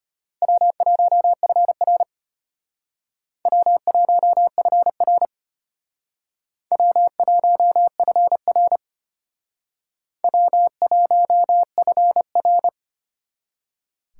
Morse code, heard as a keyed tone, about 32 wpm, 700 Hz. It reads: W1FR W1FR W1FR W1FR